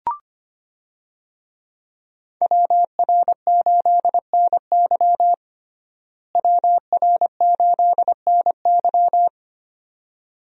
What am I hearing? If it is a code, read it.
Morse code, heard as a keyed tone, about 25 wpm, 700 Hz. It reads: WR8NY WR8NY
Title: USA Call Signs - 5 Characters - Repeated 2x - 25wpm